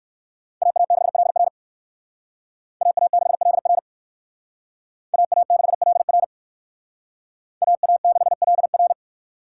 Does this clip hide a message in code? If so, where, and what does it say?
Yes: Morse code from 0.6 to 8.9 s, heard as a keyed tone, about 58 wpm, 700 Hz. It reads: AA6LR AA6LR AA6LR AA6LR